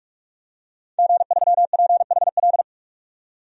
Morse code, heard as a keyed tone, about 46 wpm, 700 Hz. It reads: G3PHL
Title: DXCC-100 Call Signs - 5 Characters - 45wpm